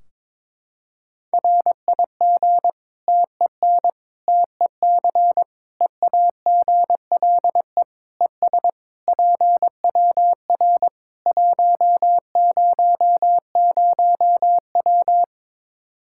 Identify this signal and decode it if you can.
Morse code, heard as a keyed tone, about 22 wpm, 700 Hz. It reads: RIG TEN TEC EAGLE ES PWR 100W
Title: Practice QSO - Exchange 2 - Info - 22wpm